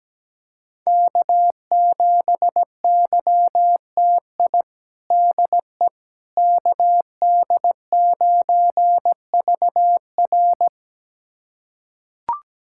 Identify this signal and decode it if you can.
Morse code, heard as a keyed tone, about 17 wpm, 700 Hz. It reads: K7YTI DE KD9VR
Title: USA Callsign DE Callsign - 5 Characters - 17wpm